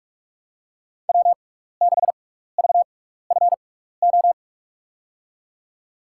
Morse code, heard as a keyed tone, about 48 wpm, spaced as for 16 wpm, 700 Hz. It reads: W6VFO
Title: USA Call Signs - 5 Characters - 6x Character Spacing - 45wpm